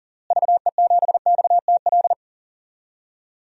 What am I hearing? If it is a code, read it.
Morse code, heard as a keyed tone, about 40 wpm, 700 Hz. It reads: VE7XTL
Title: DXCC-100 Call Signs - 6 Characters - Repeated 3x - 40wpm